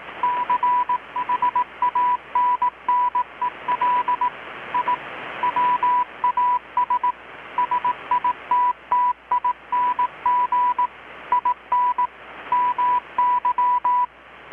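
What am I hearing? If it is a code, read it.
Morse code, heard as a keyed tone, about 18 wpm, 990 Hz. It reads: CHANNEL I WAS SITTING IN MY
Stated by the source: Morse from KPH's Night of Nights XX maritime radiotelegraphy event, received in Chiapas, Mexico about 3,800 km away on a Kenwood R-600 with a grounded 10 m long-wire antenna.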